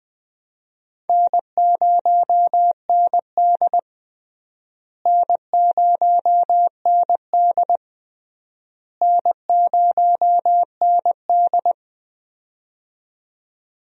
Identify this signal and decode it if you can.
Morse code, heard as a keyed tone, about 21 wpm, 700 Hz. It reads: N0ND N0ND N0ND
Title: USA Call Signs - 4 Characters - Repeated 3x - 20wpm